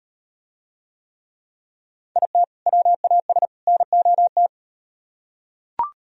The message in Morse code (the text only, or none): IT WAS NOT